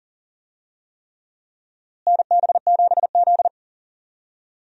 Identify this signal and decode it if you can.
Morse code, heard as a keyed tone, about 39 wpm, 700 Hz. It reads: NB7Z
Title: USA Call Signs - 4 Characters - Repeated 3x - 40wpm